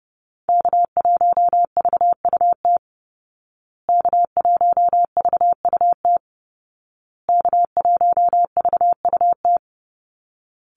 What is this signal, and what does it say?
Morse code, heard as a keyed tone, about 30 wpm, 700 Hz. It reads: K1VUT K1VUT K1VUT
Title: Call Signs Repeated 2x - 30wpm